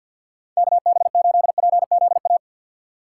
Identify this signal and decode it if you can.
Morse code, heard as a keyed tone, about 51 wpm, 700 Hz. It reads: KB8PZA
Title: USA Call Signs - 2x Character Spacing - 50wpm